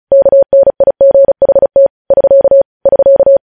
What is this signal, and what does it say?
Morse code, heard as a keyed tone, about 36 wpm, 550 Hz. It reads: KNIGHT <SK> <SK>
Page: news headlines in morse code at 35 wpm